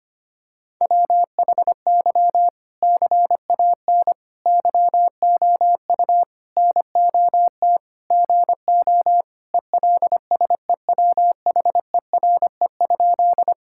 Morse code, heard as a keyed tone, about 25 wpm, 700 Hz. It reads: WHY CAN YOU NOT GO ELSEWHERE?
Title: Sentences from Top 1600 Words 25wpm